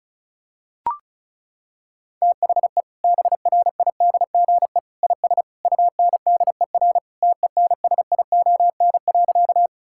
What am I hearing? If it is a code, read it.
Morse code, heard as a keyed tone, about 36 wpm, 700 Hz. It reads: THE BRIDGE IS UNDER TENSION.
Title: Sentences from Top 1600 Words 35wpm